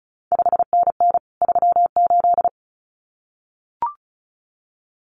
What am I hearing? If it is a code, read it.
Morse code, heard as a keyed tone, about 36 wpm, 700 Hz. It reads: 5NN 38
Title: CQ WW DX Contest Exchange Encore - 35wpm